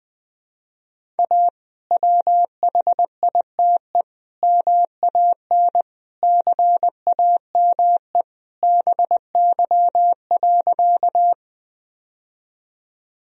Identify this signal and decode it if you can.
Morse code, heard as a keyed tone, about 21 wpm, 700 Hz. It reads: A WHITE MAN CAME BY.
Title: Sentences from Top 1000 Words - Review Part 1 - 20wpm